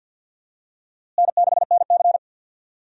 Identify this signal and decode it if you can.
Morse code, heard as a keyed tone, about 46 wpm, 700 Hz. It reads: N6NC
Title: USA Call Signs - 4 Characters - Repeated 3x - 50wpm